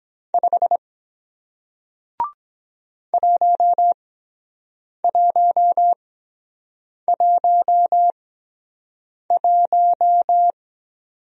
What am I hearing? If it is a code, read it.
Morse code, heard as a keyed tone, about 23 wpm, spaced as for 7 wpm, 700 Hz. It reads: 5 1111